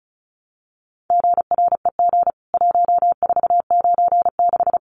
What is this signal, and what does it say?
Morse code, heard as a keyed tone, about 34 wpm, 700 Hz. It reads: GREG 1496
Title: CWT Exchange 35wpm